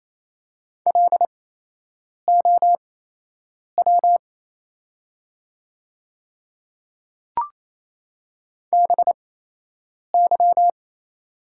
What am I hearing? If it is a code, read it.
Morse code, heard as a keyed tone, about 28 wpm, spaced as for 8 wpm, 700 Hz. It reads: LOW BY
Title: 8x Character Spacing - Top 200 Words - 28wpm